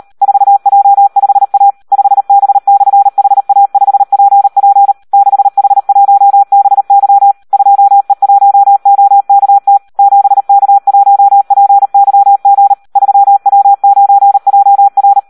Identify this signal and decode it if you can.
Morse code, heard as a keyed tone, about 38 wpm, 800 Hz. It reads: 4J5A 5B/HA5PP 6H1BY 2E1OKT 7K1PYG 3W9JR
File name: Audio practica CW / SD 1 38